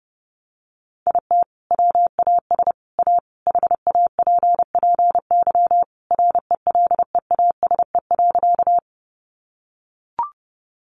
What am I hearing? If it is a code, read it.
Morse code, heard as a keyed tone, about 30 wpm, 700 Hz. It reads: IT WAS A HAPPY RELEASE.
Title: Sentences from Top 1100 Words 30wpm